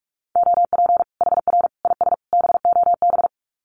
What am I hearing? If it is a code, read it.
Morse code, heard as a keyed tone, about 44 wpm, 700 Hz. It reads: OP HR IS BOB
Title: Common QSO Phrases 45wpm